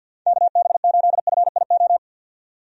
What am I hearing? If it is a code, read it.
Morse code, heard as a keyed tone, about 48 wpm, 700 Hz. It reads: KB8FIO
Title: USA Call Signs - 6 Characters - 50wpm